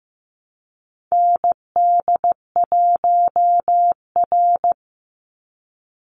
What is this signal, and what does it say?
Morse code, heard as a keyed tone, about 15 wpm, 700 Hz. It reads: ND1R
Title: Call Signs Repeated 2x - 15wpm